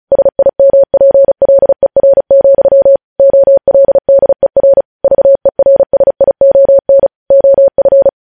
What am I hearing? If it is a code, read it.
Morse code, heard as a keyed tone, about 36 wpm, 550 Hz. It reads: SIMPLER, OLDER VERSION OF
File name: podcast / News headlines-1x-35x35